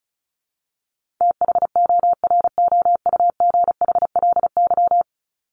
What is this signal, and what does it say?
Morse code, heard as a keyed tone, about 34 wpm, 700 Hz. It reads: THOROUGHLY
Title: LY Suffix - Top 100 Words with Base Words 35wpm